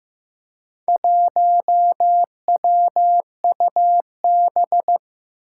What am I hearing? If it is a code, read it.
Morse code, heard as a keyed tone, about 15 wpm, 700 Hz. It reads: 1WUB
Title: Four Challenging Letter-Number Combinations 15wpm